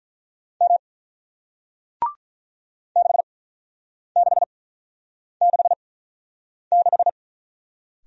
Morse code, heard as a keyed tone, about 47 wpm, spaced as for 9 wpm, 700 Hz. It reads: M 6666